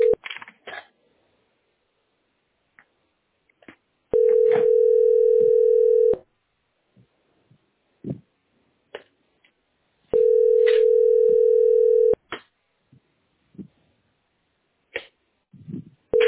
Ringback tone (North America), 440 + 480 Hz.
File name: fd8ec9f3-cf92-4b9e-9b80